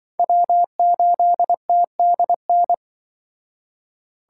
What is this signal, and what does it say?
Morse code, heard as a keyed tone, about 24 wpm, 700 Hz. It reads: W8TDN
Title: Speed Racing - USA 1x3 Call Signs 20wpm